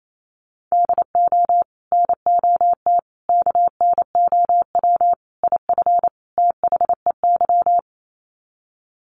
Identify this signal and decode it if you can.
Morse code, heard as a keyed tone, about 28 wpm, 700 Hz. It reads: DO NOT KNOW IF THEY
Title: Sets of 5 Words Encore 28wpm